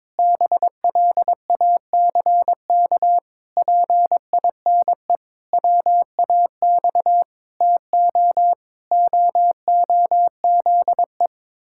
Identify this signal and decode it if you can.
Morse code, heard as a keyed tone, about 22 wpm, 700 Hz. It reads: BLACK PINE WAX TO OOZE
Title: Pangram Challenge v2 - 22wpm